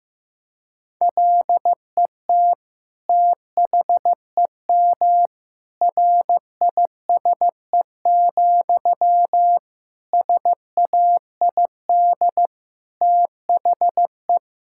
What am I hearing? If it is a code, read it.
Morse code, heard as a keyed tone, about 15 wpm, 700 Hz. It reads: LET THEM RISE, SAID THE
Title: Sentences from Top 1000 Words - Review Part 6 - 15wpm